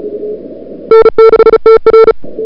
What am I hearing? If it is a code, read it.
Morse code, heard as a keyed tone, about 36 wpm, 1340 Hz, with noise. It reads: N6TR
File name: CWSprint / February 2007 CW Sprint / 01-2007 CW Sprint Hour1